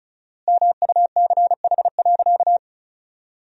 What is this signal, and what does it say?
Morse code, heard as a keyed tone, about 36 wpm, 700 Hz. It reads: MUCH.
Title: 4x Word-Spacing - Sentences from Top 500 Words - 35wpm